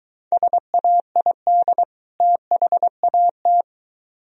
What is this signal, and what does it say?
Morse code, heard as a keyed tone, about 23 wpm, 700 Hz. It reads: SAID THAT